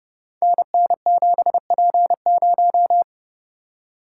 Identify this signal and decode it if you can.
Morse code, heard as a keyed tone, about 28 wpm, 700 Hz. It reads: NN7P0
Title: Code Groups - 5 Letter:Numbers - 30wpm